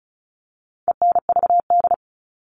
Morse code, heard as a keyed tone, about 34 wpm, 700 Hz. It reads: ENVD
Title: All Cut Numbers - 4-Digit 35wpm